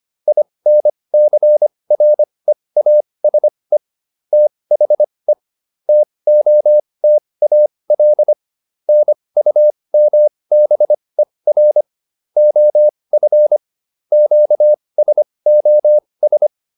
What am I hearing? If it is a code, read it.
Morse code, heard as a keyed tone, about 25 wpm, spaced as for 20 wpm, 600 Hz. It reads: INCREASE THE TOTAL NUMBER OF QSOS